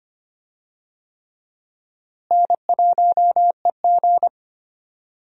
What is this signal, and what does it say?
Morse code, heard as a keyed tone, about 25 wpm, 700 Hz. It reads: N1EG
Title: USA Call Signs - 4 Characters - Repeated 3x - 25wpm